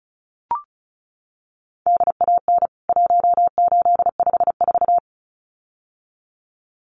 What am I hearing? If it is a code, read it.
Morse code, heard as a keyed tone, about 34 wpm, 700 Hz. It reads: DAN 1854
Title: CWT Exchange 35wpm